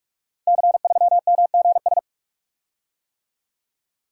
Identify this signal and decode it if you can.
Morse code, heard as a keyed tone, about 44 wpm, 700 Hz. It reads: C3MGS
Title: Code Groups - 5 Letter:Numbers - 45wpm